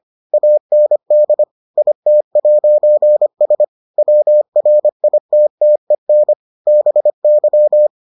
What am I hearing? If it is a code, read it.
Morse code, heard as a keyed tone, about 25 wpm, 600 Hz. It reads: AND IT'S WRITTEN BY